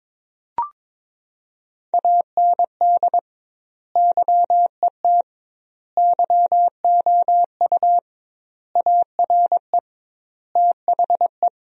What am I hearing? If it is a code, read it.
Morse code, heard as a keyed tone, about 21 wpm, 700 Hz. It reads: AND YET YOU ARE THE
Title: 2x Word-Spacing - Sentences from Top 500 Words - 22wpm